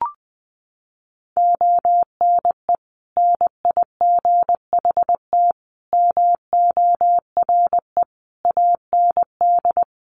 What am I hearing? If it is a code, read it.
Morse code, heard as a keyed tone, about 20 wpm, 700 Hz. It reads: ONE NIGHT MORE AND
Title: Sentences from Top 300 Words 20wpm